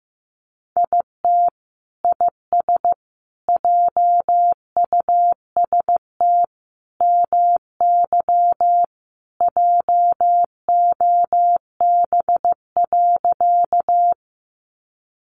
Morse code, heard as a keyed tone, about 15 wpm, 700 Hz. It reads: IT IS JUST MY JOB.
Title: Sentences from Top 600 Words 15wpm